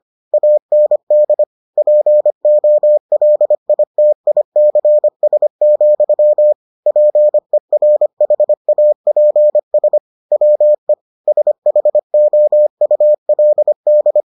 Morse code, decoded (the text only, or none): AND POLITICS, PERHAPS WE SHOULD